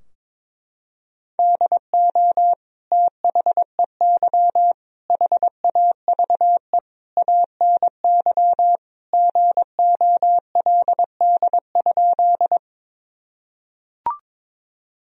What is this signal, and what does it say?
Morse code, heard as a keyed tone, about 22 wpm, 700 Hz. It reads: DO THEY HAVE ANY GOLD?
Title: Sentences from Top 1000 Words - Review Part 3 - 22wpm